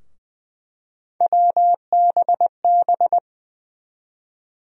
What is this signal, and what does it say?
Morse code, heard as a keyed tone, about 20 wpm, 700 Hz. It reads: WBB
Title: USA Call Sign Postfixes - 3 Letters 20wpm